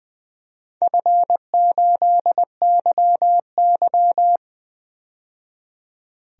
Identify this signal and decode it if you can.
Morse code, heard as a keyed tone, about 20 wpm, 700 Hz. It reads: F8YY
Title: DXCC-100 Call Signs - 4 Characters - 20wpm